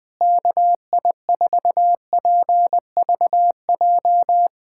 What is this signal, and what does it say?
Morse code, heard as a keyed tone, about 21 wpm, 700 Hz. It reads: KI4PVJ
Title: USA Callsign DE Callsign - 6 Characters - 20wpm